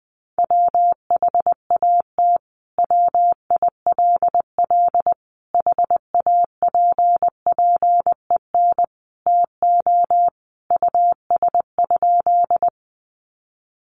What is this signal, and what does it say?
Morse code, heard as a keyed tone, about 20 wpm, 700 Hz. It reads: WHAT WILL HAPPEN TO US?
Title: Sentences from Top 400 Words 20wpm